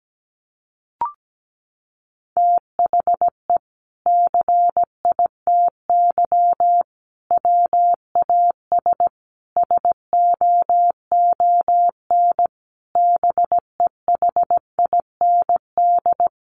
Morse code, heard as a keyed tone, about 17 wpm, 700 Hz. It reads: THE CITY WAS SOON BEHIND
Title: Sentences from Top 500 Words 17wpm